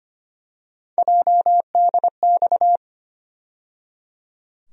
Morse code, heard as a keyed tone, about 25 wpm, 700 Hz. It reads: JDX
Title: USA Call Sign Postfixes - 3 Letters 25wpm